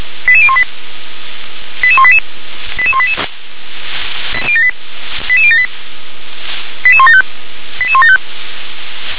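Five-tone selective call (ZVEI), heard as quick runs of five tones, tones of about 70 ms. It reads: 88018 88188 88188 88078 88175 88175
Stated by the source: ZVEI five-tone call 88185 received by SDR.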